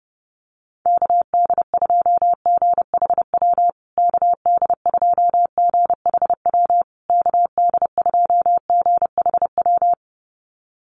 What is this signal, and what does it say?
Morse code, heard as a keyed tone, about 30 wpm, 700 Hz. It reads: KD2GHW KD2GHW KD2GHW
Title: USA Call Signs Repeated 3x 30wpm